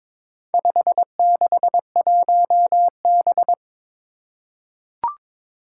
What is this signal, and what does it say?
Morse code, heard as a keyed tone, about 22 wpm, 700 Hz. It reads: 561B
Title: Four Challenging Letter-Number Combinations 22wpm